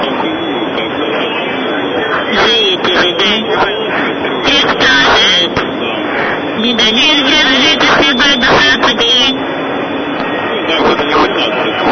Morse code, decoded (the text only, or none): none